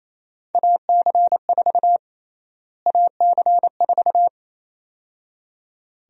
Morse code, heard as a keyed tone, about 27 wpm, 700 Hz. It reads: AC4 AC4
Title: USA Call Sign Prefixes - Repeated 2x - 28wpm